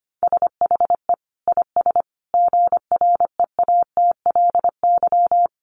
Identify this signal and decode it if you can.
Morse code, heard as a keyed tone, about 25 wpm, 700 Hz. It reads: SHE IS GREATLY
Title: Sentences from Top 1400 Words 25wpm